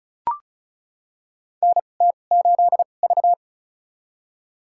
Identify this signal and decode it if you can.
Morse code, heard as a keyed tone, about 35 wpm, spaced as for 25 wpm, 690 Hz. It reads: NT8V
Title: USA Call Signs - 4 Characters - 2x Character Spacing - 35wpm